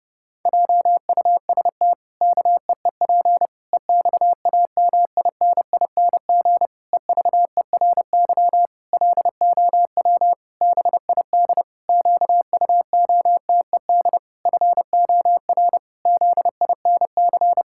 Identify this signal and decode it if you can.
Morse code, heard as a keyed tone, about 30 wpm, 700 Hz. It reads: JUST KEEP EXAMINING EVERY LOW BID QUOTED FOR ZINC
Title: Pangram Challenge v2 - 30wpm